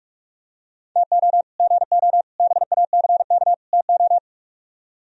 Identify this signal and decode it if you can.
Morse code, heard as a keyed tone, about 43 wpm, 700 Hz. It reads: TO GO BACK TO